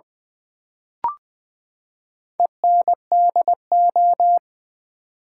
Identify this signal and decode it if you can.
Morse code, heard as a keyed tone, about 20 wpm, 700 Hz. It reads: ENDO
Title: Build A Word - 4 Letters At A Time - 11-Letter Words - 20wpm